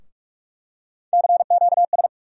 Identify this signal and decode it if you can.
Morse code, heard as a keyed tone, about 46 wpm, 700 Hz. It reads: CQS
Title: USA Call Sign Postfixes - 3 Letters 45wpm